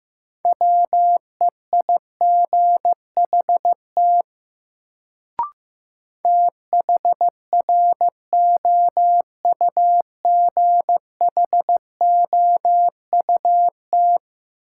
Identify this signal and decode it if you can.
Morse code, heard as a keyed tone, about 15 wpm, 700 Hz. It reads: WEIGHT THROUGHOUT
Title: Sentences from Top 800 Words - New Word Drill - 15wpm